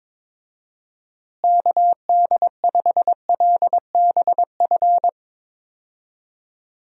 Morse code, heard as a keyed tone, about 22 wpm, 700 Hz. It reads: KD5LBF
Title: USA Call Signs - 6 Characters - Repeated 3x - 22wpm